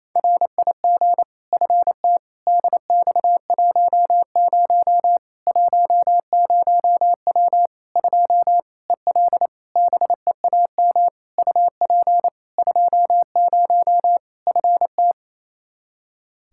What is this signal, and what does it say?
Morse code, heard as a keyed tone, about 28 wpm, 700 Hz. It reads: RIG FT DX10 10W 2 EL BEAM UP 20 FT
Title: Practice QSO - Exchange 2 - Info - 28wpm